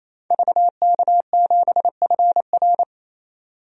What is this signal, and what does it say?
Morse code, heard as a keyed tone, about 28 wpm, 700 Hz. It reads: VK7FR
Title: DXCC-100 Call Signs - 5 Characters - 28wpm